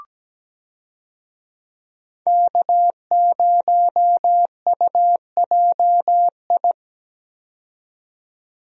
Morse code, heard as a keyed tone, about 17 wpm, 700 Hz. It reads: K0UJI
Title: USA Call Signs - 5 Characters - 17wpm